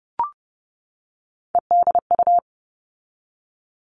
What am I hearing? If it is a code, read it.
Morse code, heard as a keyed tone, about 30 wpm, 700 Hz. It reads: EDU
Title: All Cut Numbers - 3-Digit 30wpm